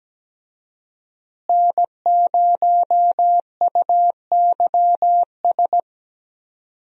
Morse code, heard as a keyed tone, about 17 wpm, 700 Hz. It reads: N0UYS
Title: USA Call Signs - 5 Characters - Repeated 3x - 17wpm